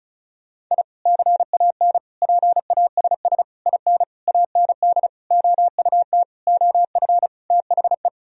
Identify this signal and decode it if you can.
Morse code, heard as a keyed tone, about 34 wpm, 700 Hz. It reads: I CAN PASS IN AND OUT OF THE
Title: Sentences from Top 1000 Words - Review Part 4 - 35wpm